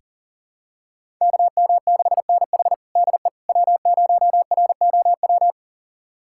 Morse code, heard as a keyed tone, about 40 wpm, 700 Hz. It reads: KM6NH DE W0ROW
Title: USA Callsign DE Callsign - 5 Characters - 40wpm